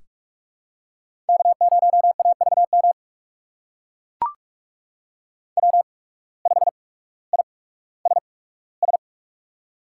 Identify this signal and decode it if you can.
Morse code, heard as a keyed tone, about 43 wpm, spaced as for 19 wpm, 700 Hz. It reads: K0AVM W5ISS
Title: USA Call Signs - 5 Characters - 8x Character Spacing - 45wpm